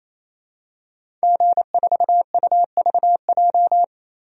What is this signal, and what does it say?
Morse code, heard as a keyed tone, about 27 wpm, 700 Hz. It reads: G4UVJ
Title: DXCC-100 Call Signs - 5 Characters - Repeated 3x - 28wpm